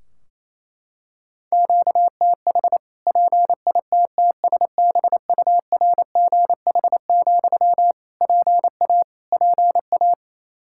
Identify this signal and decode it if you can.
Morse code, heard as a keyed tone, about 28 wpm, 700 Hz. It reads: QTH PITTSBURGH, PA PA
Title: Practice QSO - Exchange 1 - Beginning - 28wpm